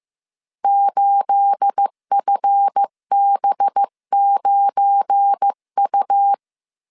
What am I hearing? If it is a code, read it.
Morse code, heard as a keyed tone, about 15 wpm, 800 Hz. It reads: 8FB9U
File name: CW / DET / 1 9 1500C S B 14wpm 3302